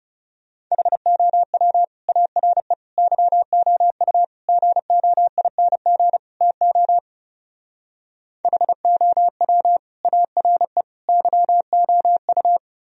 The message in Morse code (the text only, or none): HOW ARE YOU GOING TO HOW ARE YOU